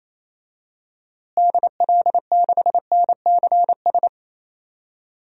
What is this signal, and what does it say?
Morse code, heard as a keyed tone, about 27 wpm, 700 Hz. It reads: DL6NCS
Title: DXCC-100 Call Signs - 6 Characters - Repeated 2x - 28wpm